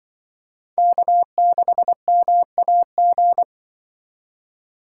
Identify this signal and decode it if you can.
Morse code, heard as a keyed tone, about 23 wpm, 700 Hz. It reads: K6MAG
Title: Speed Racing - USA 1x3 Call Signs 20wpm